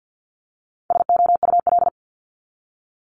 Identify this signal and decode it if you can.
Morse code, heard as a keyed tone, about 50 wpm, 700 Hz. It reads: SOUL
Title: Multi-letter Phonemes Repeated 3x 50wpm